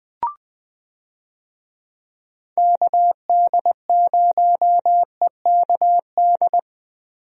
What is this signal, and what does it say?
Morse code, heard as a keyed tone, about 21 wpm, 700 Hz. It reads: KD0EKD
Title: USA Call Signs - 6 Characters - Repeated 3x - 20wpm